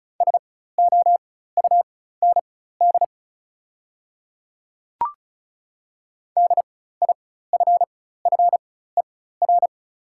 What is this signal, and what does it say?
Morse code, heard as a keyed tone, about 36 wpm, spaced as for 16 wpm, 690 Hz. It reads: SOUND DIFFER
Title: 4x Character Spacing - Top 200 Words - 35wpm